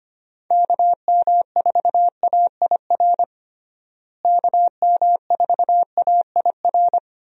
Morse code, heard as a keyed tone, about 25 wpm, 700 Hz. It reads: KM4AIR KM4AIR
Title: USA Call Signs - 6 Characters - Repeated 2x - 25wpm